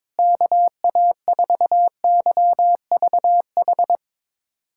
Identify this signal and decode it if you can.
Morse code, heard as a keyed tone, about 22 wpm, 700 Hz. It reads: KA4YVH
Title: USA Callsign DE Callsign 22wpm